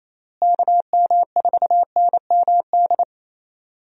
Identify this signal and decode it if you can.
Morse code, heard as a keyed tone, about 28 wpm, 700 Hz. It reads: KM4NMD
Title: USA Call Signs - 6 Characters - Repeated 2x - 28wpm